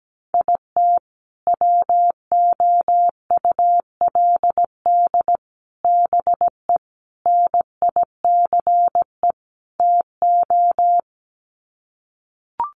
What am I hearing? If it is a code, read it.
Morse code, heard as a keyed tone, about 17 wpm, 700 Hz. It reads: IT WOULD BE NICE TO
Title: Sets of 5 Words Encore2 17wpm